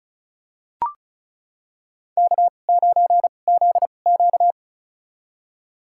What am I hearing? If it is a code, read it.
Morse code, heard as a keyed tone, about 36 wpm, spaced as for 25 wpm, 700 Hz. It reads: K9ZQ
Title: USA Call Signs - 4 Characters - 2x Character Spacing - 35wpm